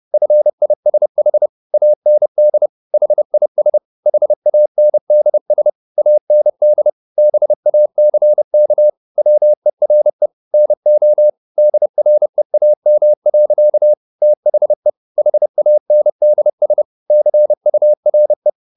Morse code, heard as a keyed tone, about 30 wpm, 600 Hz. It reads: FISH AND HIS HANDS AND BACK WERE NO DREAM. THE HANDS CURE